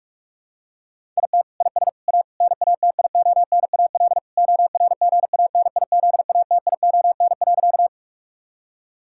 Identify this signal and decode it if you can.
Morse code, heard as a keyed tone, about 41 wpm, 700 Hz. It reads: IT IS A NATIONAL ORGANIZATION.
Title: Sentences from Top 1000 Words - Review Part 5 - 45wpm